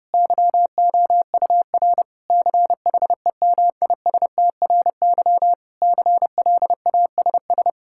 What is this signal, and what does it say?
Morse code, heard as a keyed tone, about 30 wpm, 700 Hz. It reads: YOUR CHEMISTRY CLASS
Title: Sentences from Top 1000 Words - Review Part 6 - 30wpm